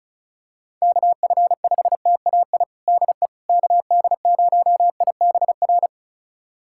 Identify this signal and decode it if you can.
Morse code, heard as a keyed tone, about 36 wpm, 700 Hz. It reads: KF5TAI DE KD0IBR
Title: USA Callsign DE Callsign 35wpm